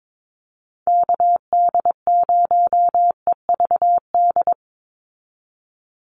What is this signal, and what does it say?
Morse code, heard as a keyed tone, about 22 wpm, 700 Hz. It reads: KD0EVD
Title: USA Call Signs 22wpm